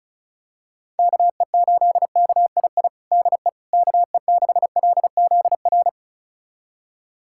Morse code, heard as a keyed tone, about 36 wpm, 700 Hz. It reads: KE8KII DE KE6LZR
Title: USA Callsign DE Callsign - 6 Characters - 35wpm